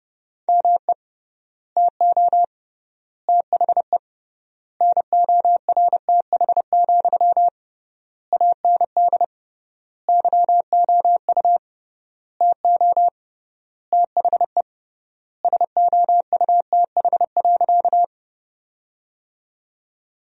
Morse code, heard as a keyed tone, about 30 wpm, 700 Hz. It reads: ME TO THE NORTH, AND YOU TO THE SOUTH.
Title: 3x Word-Spacing - Sentences from Top 400 Words - 30wpm